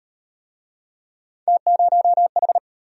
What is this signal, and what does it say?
Morse code, heard as a keyed tone, about 31 wpm, 700 Hz. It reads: T0H
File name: Speed-Racing - Three Letter-Number Combinations - 25wpm